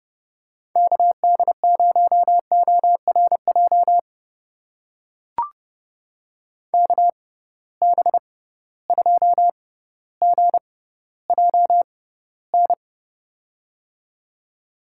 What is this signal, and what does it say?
Morse code, heard as a keyed tone, about 32 wpm, spaced as for 16 wpm, 700 Hz. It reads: KD0ORJ KB2GJN
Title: USA Call Signs - 6 Characters - 6x Character Spacing - 30wpm